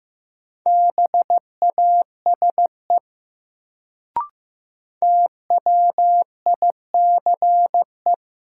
Morse code, heard as a keyed tone, about 15 wpm, 700 Hz. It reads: BASE TWICE
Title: Sentences from Top 1600 Words - New Word Drill Alt - 15wpm